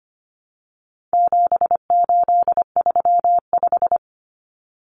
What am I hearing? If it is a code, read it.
Morse code, heard as a keyed tone, about 25 wpm, 700 Hz. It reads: 7835
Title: Numbers - 4-Digit 25wpm